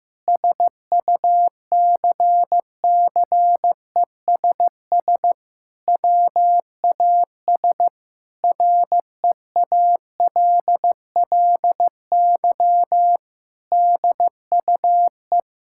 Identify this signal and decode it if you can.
Morse code, heard as a keyed tone, about 16 wpm, 700 Hz. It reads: SUCCESS WAS REALLY DUE
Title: Sentences from Top 1000 Words - Review Part 6 - 15wpm